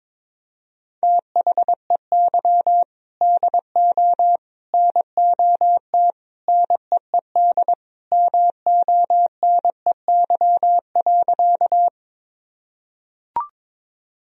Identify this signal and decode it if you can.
Morse code, heard as a keyed tone, about 22 wpm, 700 Hz. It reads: THEY DO NOT NEED MONEY.
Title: Sentences from Top 1000 Words - Review Part 1 - 22wpm